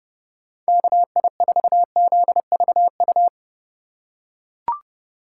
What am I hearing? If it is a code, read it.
Morse code, heard as a keyed tone, about 30 wpm, 700 Hz. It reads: KI4ZVU
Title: K1USN SST QSO Practice - 30wpm